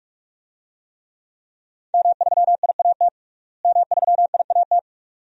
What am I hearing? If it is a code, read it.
Morse code, heard as a keyed tone, about 43 wpm, 700 Hz. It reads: M3IAT M3IAT
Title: DXCC-100 Call Signs - 5 Characters - Repeated 2x - 45wpm